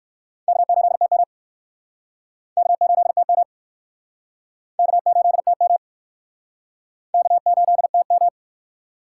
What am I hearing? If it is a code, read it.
Morse code, heard as a keyed tone, about 63 wpm, 700 Hz. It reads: K8TM K8TM K8TM K8TM